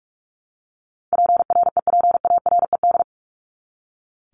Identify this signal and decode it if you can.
Morse code, heard as a keyed tone, about 44 wpm, 700 Hz. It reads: PREPARED
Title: ED Suffix - Top 100 Words with Base Words 45wpm